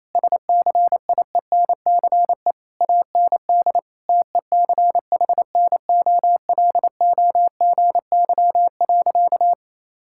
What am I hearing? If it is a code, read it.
Morse code, heard as a keyed tone, about 27 wpm, 700 Hz. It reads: SCIENCE AND TECHNOLOGY.
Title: Sentences from Top 1000 Words - Review Part 6 - 28wpm